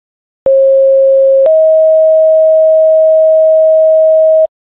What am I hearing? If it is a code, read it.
Two-tone page, heard as a one-second tone, then a three-second tone, about 540 then 640 Hz. It is heard from 0.5 to 4.5 s.